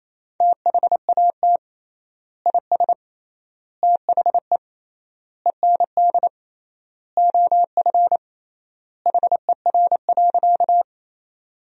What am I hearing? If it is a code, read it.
Morse code, heard as a keyed tone, about 28 wpm, 700 Hz. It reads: THAT IS THE END OF HER.
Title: 3x Word-Spacing - Sentences from Top 200 Words - 28wpm